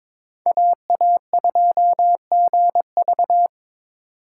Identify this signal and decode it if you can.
Morse code, heard as a keyed tone, about 22 wpm, 700 Hz. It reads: AA2GV
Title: USA Call Signs - 5 Characters - 2x Character Spacing - 22wpm